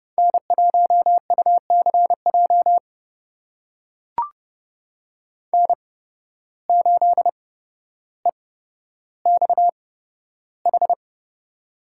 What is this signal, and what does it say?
Morse code, heard as a keyed tone, about 30 wpm, spaced as for 13 wpm, 700 Hz. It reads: N1UCJ N8EXH
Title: USA Call Signs - 5 Characters - 8x Character Spacing - 30wpm